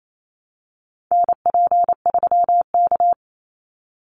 Morse code, heard as a keyed tone, about 28 wpm, 700 Hz. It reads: NP3K
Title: Call Signs Repeated 3x - 28wpm